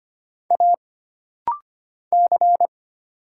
Morse code, heard as a keyed tone, about 25 wpm, 700 Hz. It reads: A C
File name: Single Character (Letter:Number:Punctuation:Prosign) - Rapid-Fire - 25wpm